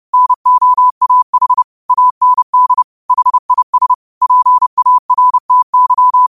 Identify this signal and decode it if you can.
Morse code, heard as a keyed tone, about 30 wpm, 1000 Hz. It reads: NOAH AND HIS PARTY
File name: qotdfiles / 30wpm